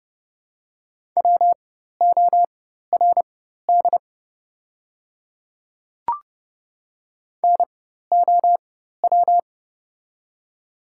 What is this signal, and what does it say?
Morse code, heard as a keyed tone, about 30 wpm, spaced as for 14 wpm, 700 Hz. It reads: WORD NOW
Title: 4x Character Spacing - Top 200 Words - 30wpm